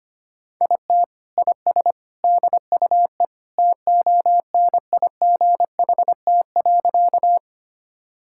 Morse code, heard as a keyed tone, about 25 wpm, 700 Hz. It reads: IT IS DUE TONIGHT.
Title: Sentences from Top 1000 Words - Review Part 6 - 25wpm